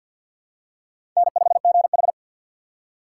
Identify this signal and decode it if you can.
Morse code, heard as a keyed tone, about 51 wpm, 700 Hz. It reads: N5GH
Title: USA Call Signs - 4 Characters - 2x Character Spacing - 50wpm